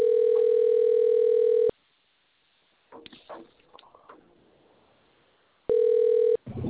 Ringback tone (North America), 440 + 480 Hz.